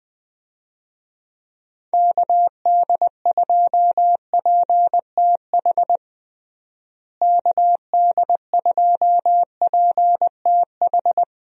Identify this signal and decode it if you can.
Morse code, heard as a keyed tone, about 20 wpm, 700 Hz. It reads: KD2PTH KD2PTH
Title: USA Call Signs - 6 Characters - Repeated 3x - 20wpm